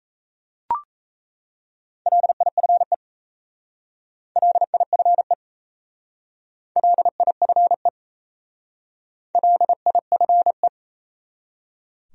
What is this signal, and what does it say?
Morse code, heard as a keyed tone, about 33 wpm, 700 Hz. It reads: LIFE LIFE LIFE LIFE
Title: Speed-Racing Top 500 Words 28wpm